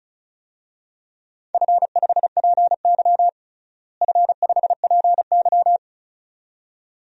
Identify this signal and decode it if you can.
Morse code, heard as a keyed tone, about 36 wpm, 700 Hz. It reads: F5PY F5PY
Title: DXCC-100 Call Signs - 4 Characters - Repeated 2x - 35wpm